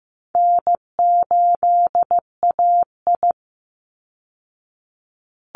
Morse code, heard as a keyed tone, about 15 wpm, 700 Hz. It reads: N8AI
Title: Easy Call Signs Repeated 3x - 15wpm